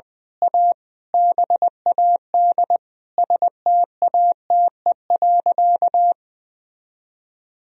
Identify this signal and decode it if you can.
Morse code, heard as a keyed tone, about 20 wpm, 700 Hz. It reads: A BAD STATE.
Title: Sentences from Top 1000 Words - Review Part 2 - 20wpm